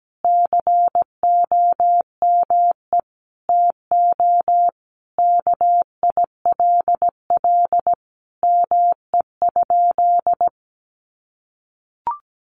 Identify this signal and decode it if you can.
Morse code, heard as a keyed tone, about 17 wpm, 700 Hz. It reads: COME TO KILL ME?
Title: Sentences from Top 1400 Words 17wpm